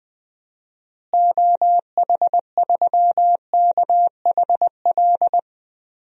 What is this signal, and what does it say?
Morse code, heard as a keyed tone, about 21 wpm, 700 Hz. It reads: OH3KHL
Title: DXCC-100 Call Signs - 6 Characters - Repeated 3x - 20wpm